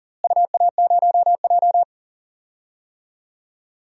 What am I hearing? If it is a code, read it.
Morse code, heard as a keyed tone, about 40 wpm, 700 Hz. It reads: UA0J
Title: DXCC-100 Call Signs - 4 Characters - Repeated 3x - 40wpm